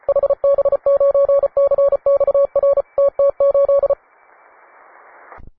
Morse code, heard as a keyed tone, about 34 wpm, 560 Hz. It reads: HB9CXR TT8
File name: wae / cw advanced / un4l cw